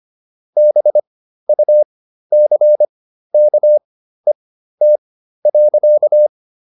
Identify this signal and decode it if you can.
Morse code, heard as a keyed tone, about 25 wpm, spaced as for 13 wpm, 600 Hz. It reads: BUCKET.